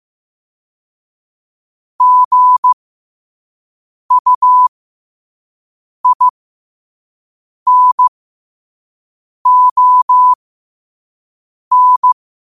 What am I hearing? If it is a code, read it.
Morse code, heard as a keyed tone, about 15 wpm, spaced as for 5 wpm, 1000 Hz. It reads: GUINON
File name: qotdfiles / 5wpm